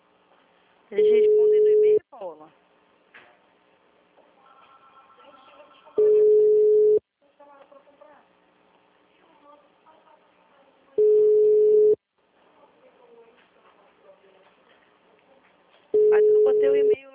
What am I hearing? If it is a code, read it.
Ringback tone (Europe, 425 Hz), with speech.